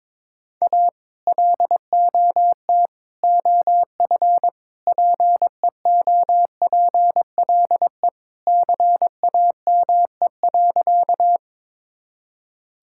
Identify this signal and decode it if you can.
Morse code, heard as a keyed tone, about 22 wpm, 700 Hz. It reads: A LOT OF PEOPLE CAME.
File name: practice / 22wpm / Sentences with 3-Word Phrases - 22wpm